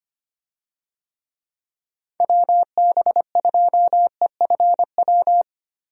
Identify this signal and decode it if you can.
Morse code, heard as a keyed tone, about 25 wpm, 700 Hz. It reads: WB2EFW
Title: USA Call Signs - 6 Characters - Repeated 3x - 25wpm